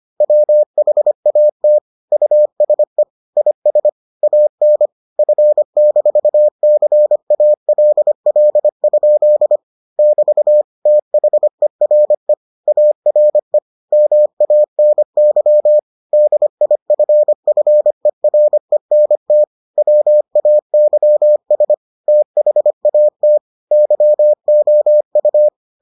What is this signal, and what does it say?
Morse code, heard as a keyed tone, about 25 wpm, 600 Hz. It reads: WHAT USE IS AN F-CALL? <BT> THERE ARE MANY DIFFERENT WAYS THAT YOU